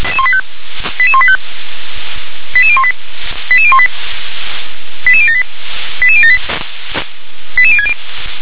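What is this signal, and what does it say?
Five-tone selective call (ZVEI), heard as quick runs of five tones, tones of about 70 ms. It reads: 88185 88018 88018 88078 88078 88060
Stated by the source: ZVEI five-tone call received by SDR.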